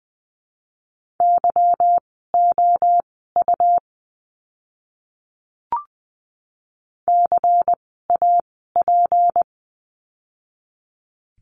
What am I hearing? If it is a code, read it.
Morse code, heard as a keyed tone, about 20 wpm, spaced as for 14 wpm, 700 Hz. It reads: YOU CAP
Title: 153 Lesson 25f - Letter p - Words 20wpm